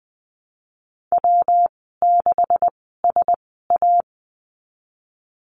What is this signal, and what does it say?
Morse code, heard as a keyed tone, about 20 wpm, spaced as for 15 wpm, 700 Hz. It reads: W6SA
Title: 147 Lesson 24g - Letter y - Call Signs 20wpm